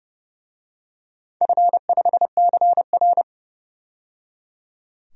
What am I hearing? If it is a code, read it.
Morse code, heard as a keyed tone, about 30 wpm, 700 Hz. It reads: F5CR
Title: DXCC-100 Call Signs - 4 Characters - 30wpm